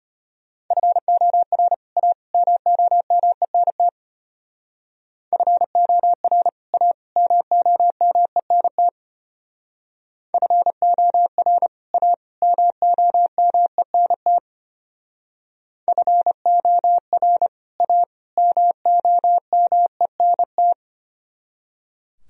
Morse code, heard as a keyed tone, about 30 wpm, 700 Hz. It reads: FOR A MOMENT FOR A MOMENT FOR A MOMENT FOR A MOMENT
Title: Speed-Racing - Sets of 3 Words 25wpm